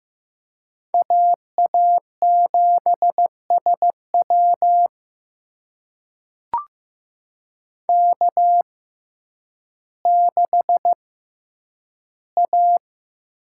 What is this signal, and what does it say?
Morse code, heard as a keyed tone, about 15 wpm, spaced as for 9 wpm, 700 Hz. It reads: AA7SW K6A
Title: USA Call Signs - 5 Characters - 6x Character Spacing - 15wpm